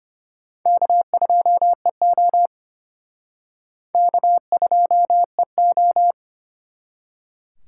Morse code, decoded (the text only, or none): K2EO K2EO